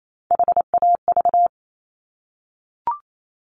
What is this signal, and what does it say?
Morse code, heard as a keyed tone, about 28 wpm, 700 Hz. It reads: HAV
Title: Top Three Letter Combinations in Words 28wpm